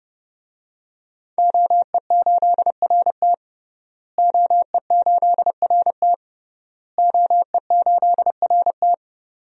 Morse code, heard as a keyed tone, about 32 wpm, 700 Hz. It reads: OE8RT OE8RT OE8RT
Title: DXCC-100 Call Signs - 5 Characters - Repeated 3x - 30wpm